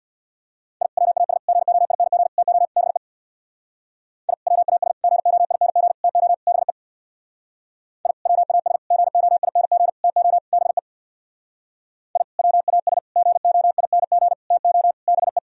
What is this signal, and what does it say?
Morse code, heard as a keyed tone, about 61 wpm, 700 Hz. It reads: I WAS GOING TO BE I WAS GOING TO BE I WAS GOING TO BE I WAS GOING TO BE
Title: Speed-Racing - Sets of 5 Words 50wpm